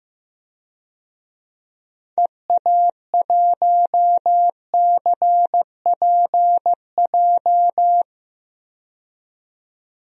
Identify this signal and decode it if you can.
Morse code, heard as a keyed tone, about 16 wpm, 700 Hz. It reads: EA1CPJ
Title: DXCC-100 Call Signs - 6 Characters - 15wpm